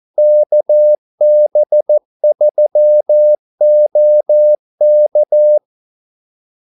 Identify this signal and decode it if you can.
Morse code, heard as a keyed tone, about 14 wpm, 600 Hz. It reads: KB3OK
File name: Callsigns-14wpm